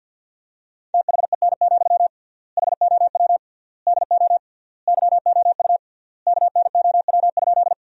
Morse code, heard as a keyed tone, about 47 wpm, 700 Hz. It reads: THEN, HOW DO YOU KNOW?
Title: 3x Word-Spacing - Sentences from Top 100 Words - 50wpm